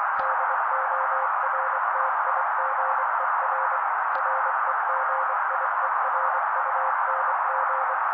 Morse code, heard as a keyed tone, about 24 wpm, 530 Hz. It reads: DORTIGER REGIERUNG
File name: daily / 23wpm noise